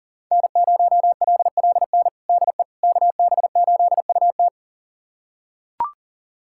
Morse code, heard as a keyed tone, about 40 wpm, 700 Hz. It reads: N0LLN DE KB8UT
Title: USA Callsign DE Callsign - 5 Characters - 40wpm